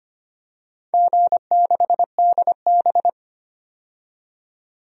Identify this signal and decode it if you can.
Morse code, heard as a keyed tone, about 25 wpm, 700 Hz. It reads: G6DB
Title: DXCC-100 Call Signs - 4 Characters - 25wpm